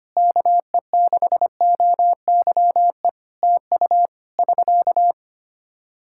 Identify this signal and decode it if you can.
Morse code, heard as a keyed tone, about 25 wpm, 700 Hz. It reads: KE6OYE TU <SK>
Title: Practice QSO - Exchange 4 - Ending - 25wpm